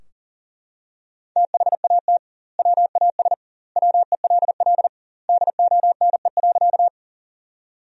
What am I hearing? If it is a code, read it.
Morse code, heard as a keyed tone, about 40 wpm, 700 Hz. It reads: THAT WAS WELL DONE.
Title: 2x Word-Spacing - Sentences from Top 500 Words - 40wpm